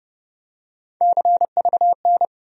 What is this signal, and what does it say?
Morse code, heard as a keyed tone, about 30 wpm, 700 Hz. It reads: CVN
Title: USA Call Sign Postfixes - 3 Letters 30wpm